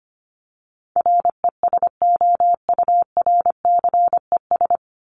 Morse code, heard as a keyed tone, about 25 wpm, 700 Hz. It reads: RESOURCES
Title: ES:S Suffix - Top 100 Words with Base Words 25wpm